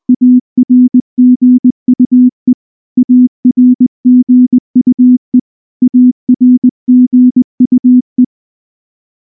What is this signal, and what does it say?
Morse code, heard as a keyed tone, about 19 wpm, 250 Hz. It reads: ARGUE ARGUE ARGUE
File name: morse / morse-training-20WPM-8